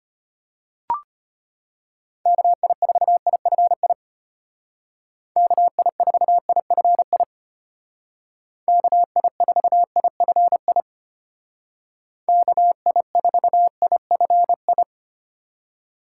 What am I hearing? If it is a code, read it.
Morse code, heard as a keyed tone, about 33 wpm, 700 Hz. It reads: KI4IFI KI4IFI KI4IFI KI4IFI